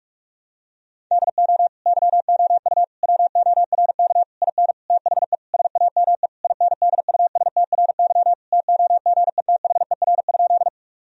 Morse code, heard as a keyed tone, about 41 wpm, 700 Hz. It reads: DO YOU WORK IN THE SAME INDUSTRY TOGETHER?
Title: Sentences from Top 1000 Words - Review Part 4 - 45wpm